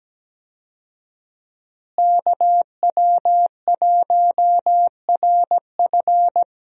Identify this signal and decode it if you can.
Morse code, heard as a keyed tone, about 17 wpm, 700 Hz. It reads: KW1RF
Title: USA Call Signs - 5 Characters - Repeated 2x - 17wpm